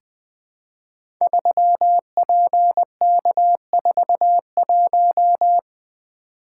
Morse code, heard as a keyed tone, about 20 wpm, 700 Hz. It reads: 3PK41
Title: Code Groups - 5 Letter:Numbers - 20wpm